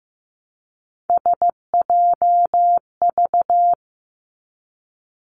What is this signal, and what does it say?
Morse code, heard as a keyed tone, about 15 wpm, 700 Hz. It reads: SJV
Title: ARRL Sections 15wpm